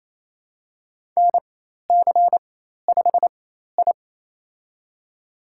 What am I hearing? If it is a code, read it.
Morse code, heard as a keyed tone, about 28 wpm, spaced as for 13 wpm, 700 Hz. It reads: NC5I
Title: USA Call Signs - 4x Character Spacing - 28wpm